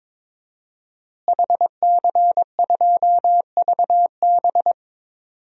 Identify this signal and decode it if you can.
Morse code, heard as a keyed tone, about 22 wpm, 700 Hz. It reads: HC2VB
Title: DXCC-100 Call Signs - 5 Characters - Repeated 2x - 22wpm